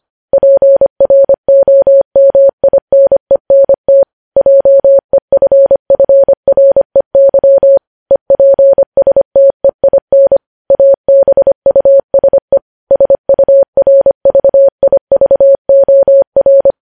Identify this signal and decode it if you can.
Morse code, heard as a keyed tone, about 25 wpm, 550 Hz. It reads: PROMINENT JEFFREY EPSTEIN ABUSE SURVIVOR
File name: podcast / News headlines-1x-25x25